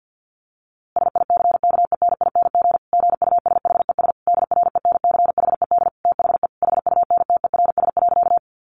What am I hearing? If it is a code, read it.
Morse code, heard as a keyed tone, about 49 wpm, 700 Hz. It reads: SICKENING GUSHES DRENCHED THE HUNTERS.
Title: A Sound Of Thunder 50wpm